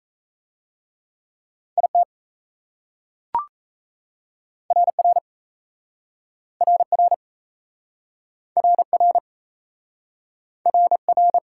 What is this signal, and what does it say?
Morse code, heard as a keyed tone, about 38 wpm, 700 Hz. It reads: IT RR RR RR RR